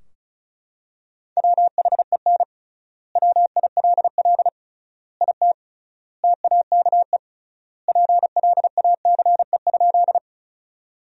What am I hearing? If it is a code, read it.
Morse code, heard as a keyed tone, about 36 wpm, 700 Hz. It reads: WHEN WILL IT TAKE PLACE?
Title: 3x Word-Spacing - Sentences from Top 200 Words - 35wpm